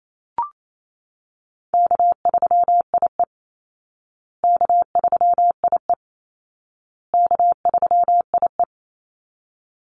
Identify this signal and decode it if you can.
Morse code, heard as a keyed tone, about 28 wpm, 700 Hz. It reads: K3IE K3IE K3IE
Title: Easy Call Signs Repeated 3x - 28wpm